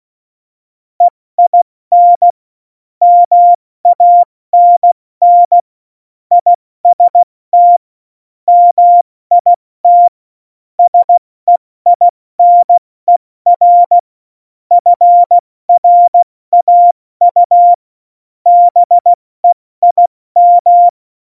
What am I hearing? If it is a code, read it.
Morse code, heard as a keyed tone, about 16 wpm, 700 Hz. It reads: EIN MANN IST MIT SEINER FRAU BEIM
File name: Morsekurs / Witze / Witz 13 March 2018, 70 F80